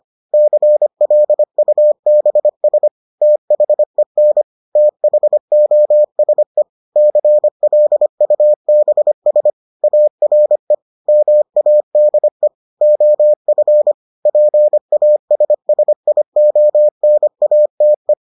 Morse code, heard as a keyed tone, about 25 wpm, 600 Hz. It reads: CLUBS THEN THOSE CLUBS ARE MADE OF PASSIONATE